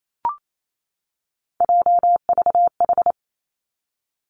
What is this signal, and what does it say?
Morse code, heard as a keyed tone, about 28 wpm, 700 Hz. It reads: JVH
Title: Three Challenging Letter-Number Combinations 28wpm